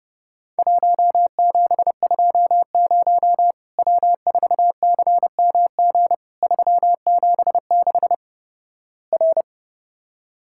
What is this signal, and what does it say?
Morse code, heard as a keyed tone, about 29 wpm, 700 Hz. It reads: 1720 W4CMG 376 R
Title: Worked All Europe DX Contest - QTC Exchange - 30wpm